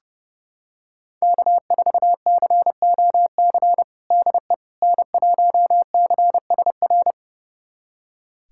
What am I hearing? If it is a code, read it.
Morse code, heard as a keyed tone, about 31 wpm, 700 Hz. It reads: K4COC DE N1CSR
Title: USA Callsign DE Callsign 30wpm